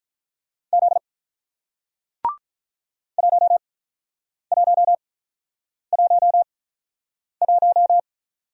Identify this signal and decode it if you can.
Morse code, heard as a keyed tone, about 48 wpm, 700 Hz. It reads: Z 1 1 1 1